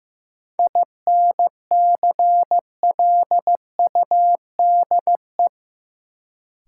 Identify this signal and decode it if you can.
Morse code, heard as a keyed tone, about 15 wpm, 700 Hz. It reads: INCLUDE
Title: Sentences from Top 1000 Words - Review Part 3 - New Word Drill - 15wpm